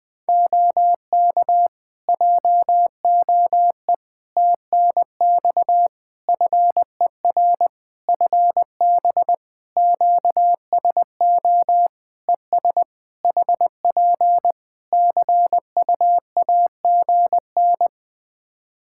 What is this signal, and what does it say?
Morse code, heard as a keyed tone, about 20 wpm, 700 Hz. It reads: OK JOE TNX FER FB QSO ES HP CUAGN
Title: K1USN SST QSO Practice 20wpm